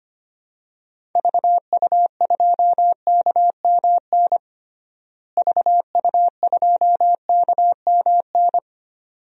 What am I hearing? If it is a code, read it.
Morse code, heard as a keyed tone, about 25 wpm, 700 Hz. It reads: VU2KMN VU2KMN
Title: DXCC-100 Call Signs - 6 Characters - Repeated 2x - 25wpm